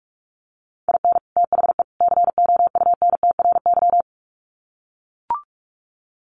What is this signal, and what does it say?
Morse code, heard as a keyed tone, about 44 wpm, 700 Hz. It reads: IN THE COUNTRY
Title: Sets of 3 Words 45wpm